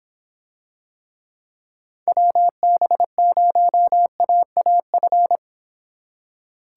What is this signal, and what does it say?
Morse code, heard as a keyed tone, about 26 wpm, 700 Hz. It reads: WB0AAF